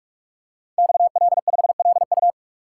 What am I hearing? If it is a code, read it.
Morse code, heard as a keyed tone, about 44 wpm, 700 Hz. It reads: XL5LU
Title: Code Groups - 5 Letter:Numbers - 45wpm